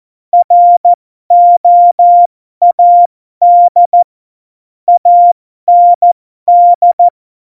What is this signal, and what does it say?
Morse code, heard as a keyed tone, about 14 wpm, spaced as for 12 wpm, 700 Hz. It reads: ROAD AND
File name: Morsekurs / Witze